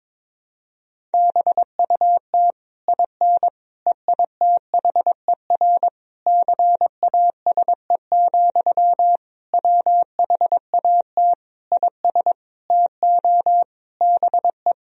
Morse code, heard as a keyed tone, about 22 wpm, 700 Hz. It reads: BUT IN EITHER CASE, WHAT IS TO BE
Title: Sentences from Top 1000 Words - Review Part 4 - 22wpm